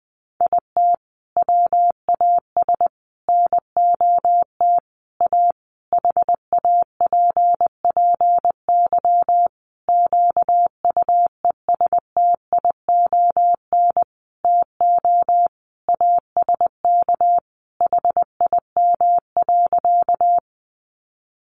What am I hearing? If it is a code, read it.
Morse code, heard as a keyed tone, about 20 wpm, 700 Hz. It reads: IT WAS NOT A HAPPY QUESTION TO ASK HIM.
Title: Sentences from Top 1000 Words 20wpm